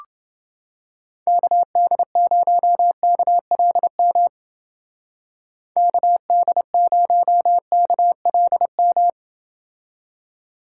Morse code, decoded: KD0KLM KD0KLM

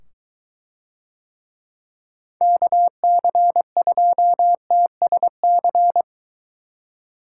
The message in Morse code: KC2TSC

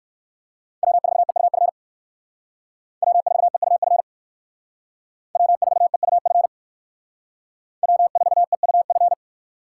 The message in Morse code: W4EUF W4EUF W4EUF W4EUF

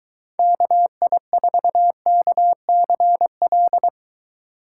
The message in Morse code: KI4KCL

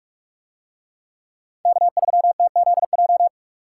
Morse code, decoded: K3TZJ

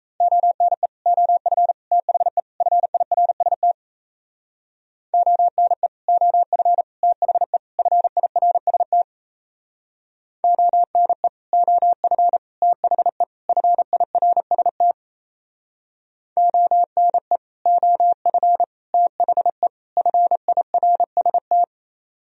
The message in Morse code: ONE OF THE FIRST ONE OF THE FIRST ONE OF THE FIRST ONE OF THE FIRST

